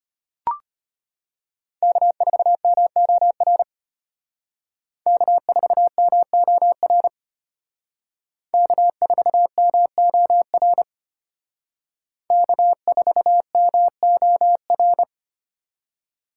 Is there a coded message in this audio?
Yes: Morse code from 1.8 to 15.0 s, heard as a keyed tone, about 32 wpm, 700 Hz. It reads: K4MOR K4MOR K4MOR K4MOR